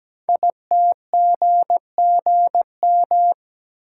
Morse code, heard as a keyed tone, about 17 wpm, 700 Hz. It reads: ITGGM
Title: Code Groups - 5 Letters - 17wpm